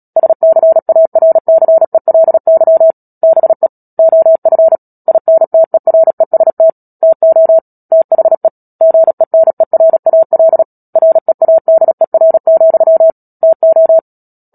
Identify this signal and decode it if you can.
Morse code, heard as a keyed tone, about 37 wpm, 650 Hz. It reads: SCARCELY BE OF INTEREST TO THE GENERAL READER, TO
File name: Book Two / 10-Chapter 10 - Epilogue - 36 WPM